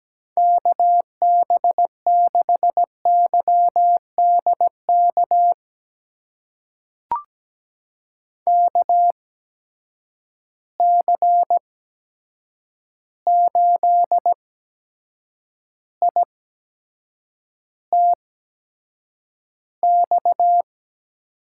Morse code, heard as a keyed tone, about 17 wpm, spaced as for 7 wpm, 700 Hz. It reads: KB6YDK KC8ITX
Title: USA Call Signs - 6 Characters - 8x Character Spacing - 17wpm